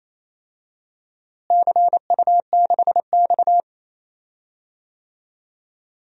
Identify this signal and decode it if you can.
Morse code, heard as a keyed tone, about 28 wpm, 700 Hz. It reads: CU6X
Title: DXCC-100 Call Signs - 4 Characters - Repeated 2x - 28wpm